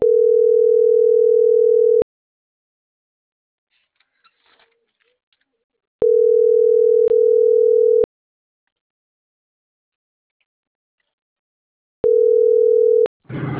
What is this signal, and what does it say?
Ringback tone (North America), 440 + 480 Hz.